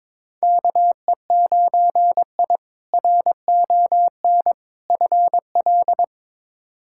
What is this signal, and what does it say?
Morse code, heard as a keyed tone, about 22 wpm, 700 Hz. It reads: KE9I RON FL
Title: K1USN SST QSO Practice - 22wpm